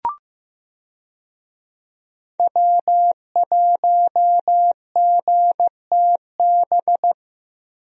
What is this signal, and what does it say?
Morse code, heard as a keyed tone, about 15 wpm, 700 Hz. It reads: W1GTB
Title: USA Call Signs - 5 Characters - Repeated 2x - 15wpm